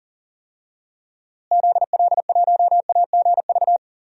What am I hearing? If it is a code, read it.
Morse code, heard as a keyed tone, about 39 wpm, 700 Hz. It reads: ZL1AGV
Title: DXCC-100 Call Signs - 6 Characters - Repeated 3x - 40wpm